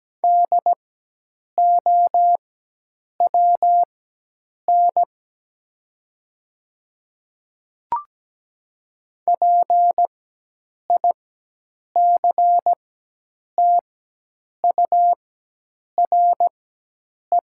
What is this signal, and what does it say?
Morse code, heard as a keyed tone, about 17 wpm, spaced as for 8 wpm, 700 Hz. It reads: DOWN PICTURE
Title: 4x Character Spacing - Top 300 Words - 17wpm